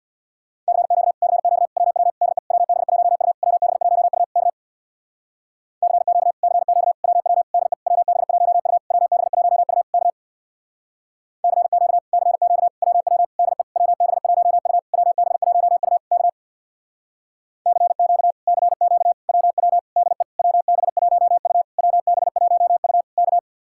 Morse code, heard as a keyed tone, about 64 wpm, 700 Hz. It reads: CQ CQ WW DE WB1U WB1U K CQ CQ WW DE WB1U WB1U K CQ CQ WW DE WB1U WB1U K CQ CQ WW DE WB1U WB1U K